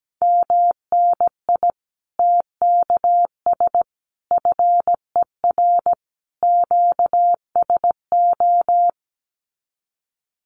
Morse code, heard as a keyed tone, about 17 wpm, 700 Hz. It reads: MNI TKS FER QSO
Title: Common QSO Phrases 17wpm